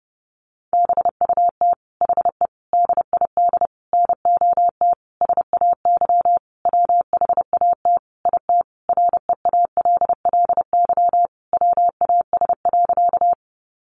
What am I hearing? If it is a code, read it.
Morse code, heard as a keyed tone, about 30 wpm, 700 Hz. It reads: BUT HE DID NOT SAY WHAT IT REALLY WAS.
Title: Sentences from Top 600 Words 30wpm